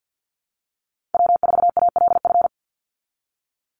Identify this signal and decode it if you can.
Morse code, heard as a keyed tone, about 50 wpm, 700 Hz. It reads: W4ALF
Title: Call Signs Repeated 2x - 50wpm